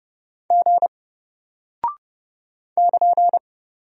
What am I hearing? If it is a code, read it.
Morse code, heard as a keyed tone, about 30 wpm, 700 Hz. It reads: G <KN>
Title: Single Character (Letter:Number:Punctuation:Prosign) Alt 30wpm